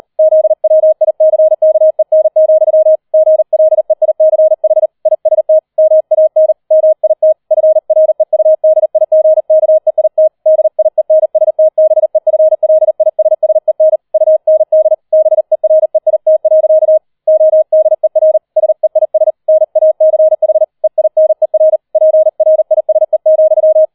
Morse code, heard as a keyed tone, about 38 wpm, 630 Hz. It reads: ZWICKEN, GLEICH IST MAN MIT FREUDIGKEIT DIENSTBEFLISSEN UND BEREIT. ODER SEIS NACH EINER PRISE,
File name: 40wpm